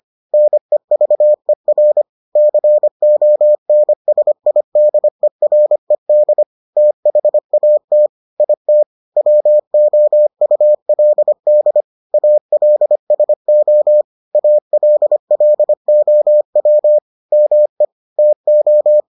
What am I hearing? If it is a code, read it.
Morse code, heard as a keyed tone, about 25 wpm, 600 Hz. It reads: NEVER CONSIDERED THAT IT WOULD ALSO ALLOW ME TO